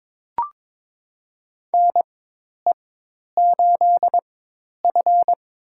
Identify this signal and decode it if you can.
Morse code, heard as a keyed tone, about 22 wpm, spaced as for 10 wpm, 700 Hz. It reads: NE8F
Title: USA Call Signs - 4 Characters - 4x Character Spacing - 22wpm